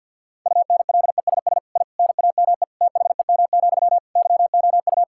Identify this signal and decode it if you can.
Morse code, heard as a keyed tone, about 50 wpm, 700 Hz. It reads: UNLESS I NAME THEM, YOU
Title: Sentences from Top 1000 Words - Review Part 6 - 50wpm